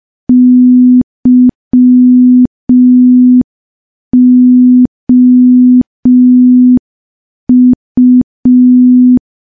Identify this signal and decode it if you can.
Morse code, heard as a keyed tone, about 5 wpm, 250 Hz. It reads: YOU
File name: morse / morse-training-05WPM